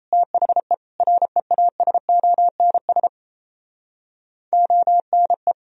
Morse code, heard as a keyed tone, about 37 wpm, 700 Hz. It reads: THE REASONS ONE